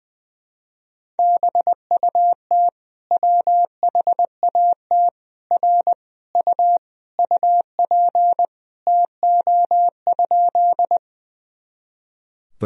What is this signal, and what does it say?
Morse code, heard as a keyed tone, about 20 wpm, 700 Hz. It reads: BUT WHAT R U UP TO?
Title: Sentences from Top 100 Words and Abbreviations 20wpm